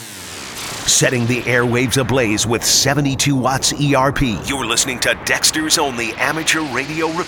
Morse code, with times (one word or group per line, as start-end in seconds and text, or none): none